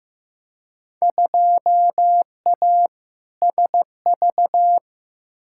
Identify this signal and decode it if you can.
Morse code, heard as a keyed tone, about 15 wpm, 700 Hz. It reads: 2A SV
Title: Field Day Exchanges - 15wpm